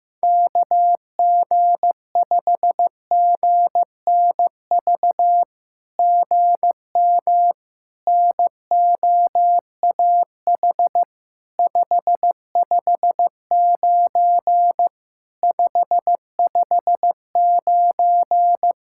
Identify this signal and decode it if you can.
Morse code, heard as a keyed tone, about 15 wpm, 700 Hz. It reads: KG5GNV GM NOAH 559 559
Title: SOTA QSO Practice 15wpm